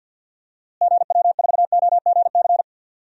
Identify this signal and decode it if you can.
Morse code, heard as a keyed tone, about 50 wpm, 700 Hz. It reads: GW4OGC
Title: DXCC-100 Call Signs - 6 Characters - Repeated 3x - 50wpm